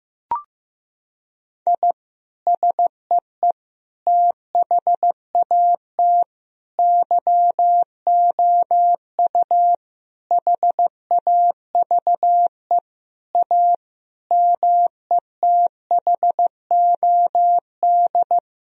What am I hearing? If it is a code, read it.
Morse code, heard as a keyed tone, about 15 wpm, 700 Hz. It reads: I SEE THAT YOU HAVE A METHOD
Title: Sentences from Top 1000 Words - Review Part 3 - 15wpm